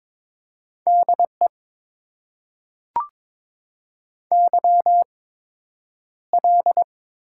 Morse code, heard as a keyed tone, about 21 wpm, spaced as for 9 wpm, 700 Hz. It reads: DE YL
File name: Top 90 QSO Elements - 8x Character Spacing - 22wpm